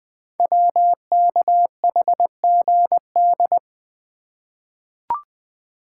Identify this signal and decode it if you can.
Morse code, heard as a keyed tone, about 20 wpm, 700 Hz. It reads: WKHGD
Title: Code Groups - 5 Letters - 20wpm